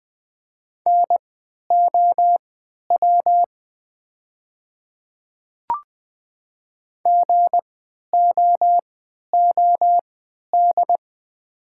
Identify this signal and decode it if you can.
Morse code, heard as a keyed tone, about 20 wpm, spaced as for 11 wpm, 700 Hz. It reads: NOW GOOD
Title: 3x Character Spacing - Top 300 Words - 20wpm